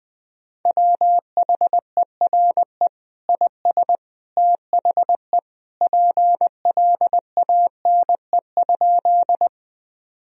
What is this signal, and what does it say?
Morse code, heard as a keyed tone, about 21 wpm, 700 Hz. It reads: WHERE IS THE PLANE?
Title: Sentences from Top 1000 Words - Review Part 3 - 20wpm